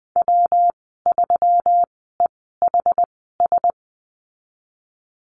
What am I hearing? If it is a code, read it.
Morse code, heard as a keyed tone, about 20 wpm, spaced as for 15 wpm, 700 Hz. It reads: W3EHS
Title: 140 Lesson 23g - Letter f - Call Signs 20wpm